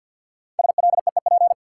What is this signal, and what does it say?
Morse code, heard as a keyed tone, about 52 wpm, 700 Hz. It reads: SLEEP
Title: Sentences from Top 500 Words - New Word Drill - 50wpm